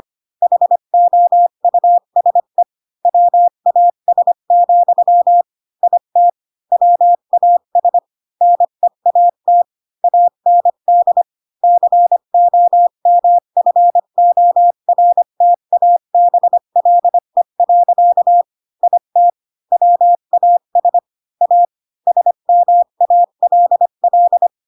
Morse code, heard as a keyed tone, about 25 wpm, 700 Hz. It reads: HOUSE WAS, IT WAS NEAT AND COMFORTABLE. IT WAS A SMALL